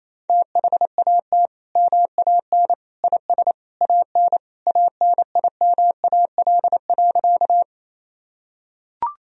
Morse code, heard as a keyed tone, about 28 wpm, 700 Hz. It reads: THAT MAN IS AN ANIMAL.
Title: Sentences from Top 1000 Words - Review Part 1 - 28wpm